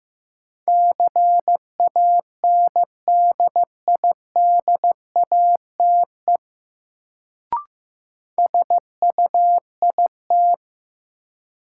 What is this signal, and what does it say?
Morse code, heard as a keyed tone, about 15 wpm, 700 Hz. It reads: CANDIDATE SUIT
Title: Sentences from Top 1600 Words - New Word Drill Alt - 15wpm